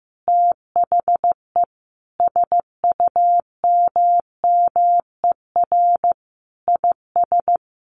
Morse code, heard as a keyed tone, about 15 wpm, 700 Hz. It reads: THE SUMMER IS
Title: Sentences from Top 1100 Words 15wpm